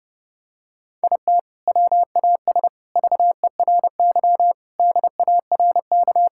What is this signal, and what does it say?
Morse code, heard as a keyed tone, about 30 wpm, 700 Hz. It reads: IT WAS VERY DARK
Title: Sentences from Top 1000 Words - Review Part 6 - 30wpm